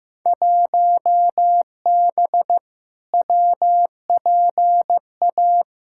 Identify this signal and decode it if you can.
Morse code, heard as a keyed tone, about 15 wpm, 700 Hz. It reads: 1B WPA
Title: Field Day Exchanges Encore - 15wpm